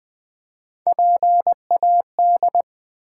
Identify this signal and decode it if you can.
Morse code, heard as a keyed tone, about 20 wpm, 700 Hz. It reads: PAD
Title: Build A Word - 1 Letter At A Time - 3-Letter Words - 20wpm